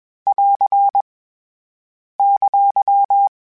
Morse code, heard as a keyed tone, about 21 wpm, 800 Hz. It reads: <AR> !
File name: (spaced) / FromSMs spaced mp3s / Antennas sp@25